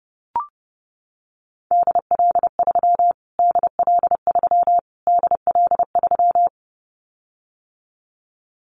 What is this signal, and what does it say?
Morse code, heard as a keyed tone, about 30 wpm, 700 Hz. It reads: DL3 DL3 DL3
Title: Callsign Prefixes for Top 25 DXCC - Repeated 3x - 30wpm